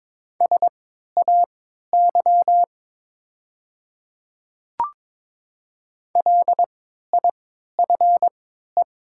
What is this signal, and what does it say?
Morse code, heard as a keyed tone, about 22 wpm, spaced as for 13 wpm, 700 Hz. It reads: SAY LIFE
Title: 3x Character Spacing - Top 300 Words - 22wpm